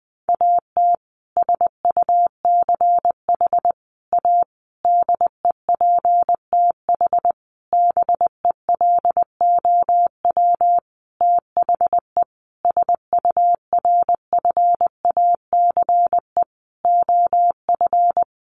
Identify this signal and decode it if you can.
Morse code, heard as a keyed tone, about 19 wpm, 700 Hz. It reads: AT SUCH A DEPTH BELOW THE SURFACE OF
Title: Sentences from Top 1200 Words 20wpm